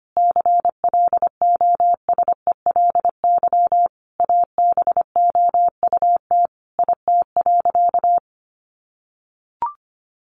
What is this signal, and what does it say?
Morse code, heard as a keyed tone, about 25 wpm, 700 Hz. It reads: CLOSELY ABOUT IT.
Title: Sentences from Top 1400 Words 25wpm